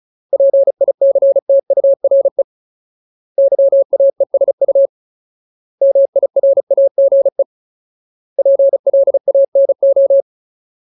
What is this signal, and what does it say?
Morse code, heard as a keyed tone, about 34 wpm, 550 Hz. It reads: PICTURE YAESU MIRAGE PLANO
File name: Ultimate Word List / 35WPM Ultimate